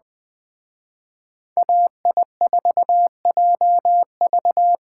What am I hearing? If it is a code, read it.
Morse code, heard as a keyed tone, about 21 wpm, 700 Hz. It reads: AI4JV
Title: Speed Racing - USA 2x2 Call Signs 17wpm